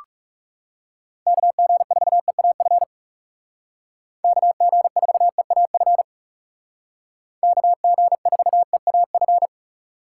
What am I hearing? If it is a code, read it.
Morse code, heard as a keyed tone, about 40 wpm, 700 Hz. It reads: KG4EAF KG4EAF KG4EAF